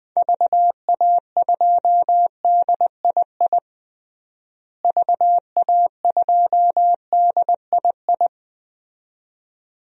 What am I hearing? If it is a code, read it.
Morse code, heard as a keyed tone, about 20 wpm, 700 Hz. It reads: VA2DII VA2DII
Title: DXCC-100 Call Signs - 6 Characters - Repeated 2x - 20wpm